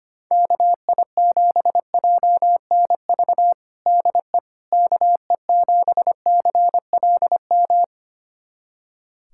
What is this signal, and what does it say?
Morse code, heard as a keyed tone, about 25 wpm, 700 Hz. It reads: KI7JNV DE KE7CLM
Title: USA Callsign DE Callsign - 6 Characters - 25wpm